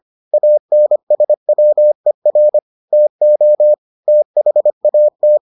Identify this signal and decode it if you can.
Morse code, heard as a keyed tone, about 25 wpm, 600 Hz. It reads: ANSWER TO THAT